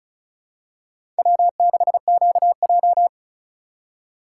Morse code, heard as a keyed tone, about 36 wpm, 700 Hz. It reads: W6QJ
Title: USA Call Signs - 4 Characters - 2x Character Spacing - 35wpm